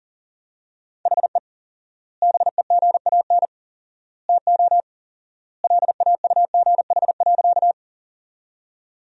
Morse code, heard as a keyed tone, about 40 wpm, 700 Hz. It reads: HE BEGAN TO LAUGH.
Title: 4x Word-Spacing - Sentences from Top 500 Words - 40wpm